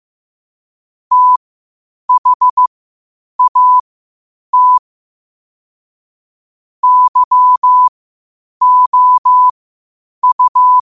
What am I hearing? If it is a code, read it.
Morse code, heard as a keyed tone, about 15 wpm, spaced as for 8 wpm, 1000 Hz. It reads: THAT YOU
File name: 8wpm